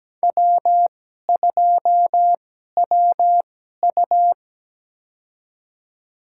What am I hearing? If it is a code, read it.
Morse code, heard as a keyed tone, about 17 wpm, spaced as for 12 wpm, 700 Hz. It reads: W2WU
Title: USA Call Signs - 2x Character Spacing - 17wpm